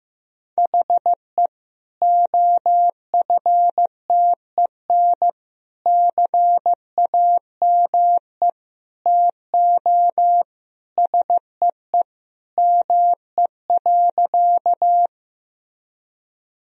Morse code, heard as a keyed tone, about 15 wpm, 700 Hz. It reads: HE OFTEN CAME TO SEE ME.
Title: Sentences from Top 1000 Words - Review Part 1 - 15wpm